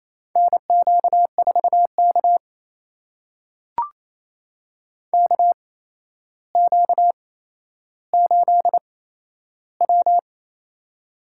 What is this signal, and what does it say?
Morse code, heard as a keyed tone, about 27 wpm, spaced as for 12 wpm, 700 Hz. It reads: NQ4K KQ8W
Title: USA Call Signs - 4 Characters - 8x Character Spacing - 28wpm